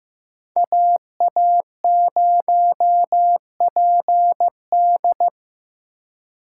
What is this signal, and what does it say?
Morse code, heard as a keyed tone, about 15 wpm, 700 Hz. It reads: AA0PD
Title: USA Call Signs - 6x Character Spacing - 15wpm